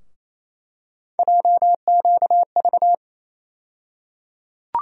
Morse code, heard as a keyed tone, about 28 wpm, 700 Hz. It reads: JQV
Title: USA Call Sign Postfixes - 3 Letters 28wpm